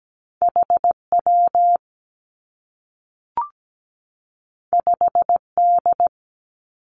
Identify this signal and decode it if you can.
Morse code, heard as a keyed tone, about 17 wpm, 700 Hz. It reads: HW 5D
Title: Two Challenging Letter-Number Combinations 17wpm